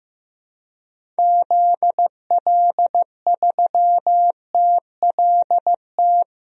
Morse code, heard as a keyed tone, about 15 wpm, 700 Hz. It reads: ZL3TLT
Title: DXCC-100 Call Signs - 6 Characters - 15wpm